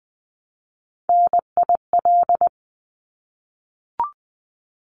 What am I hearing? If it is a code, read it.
Morse code, heard as a keyed tone, about 20 wpm, 700 Hz. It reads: NIL
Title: Abbreviations 20wpm